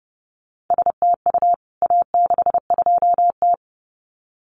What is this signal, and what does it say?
Morse code, heard as a keyed tone, about 30 wpm, 700 Hz. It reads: STU A62T
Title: CWT Exchange 30wpm